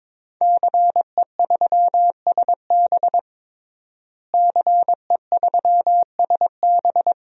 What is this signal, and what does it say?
Morse code, heard as a keyed tone, about 21 wpm, 700 Hz. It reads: CE3SB CE3SB
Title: DXCC-100 Call Signs - 5 Characters - Repeated 2x - 22wpm